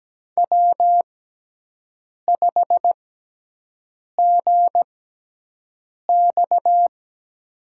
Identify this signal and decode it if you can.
Morse code, heard as a keyed tone, about 17 wpm, spaced as for 6 wpm, 700 Hz. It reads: W5GX
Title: USA Call Signs - 4 Characters - 6x Character Spacing - 17wpm